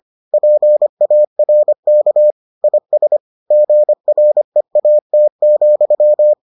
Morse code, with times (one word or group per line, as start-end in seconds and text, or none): 0.3-2.3 PARK
2.6-3.2 IS
3.5-6.3 GREAT,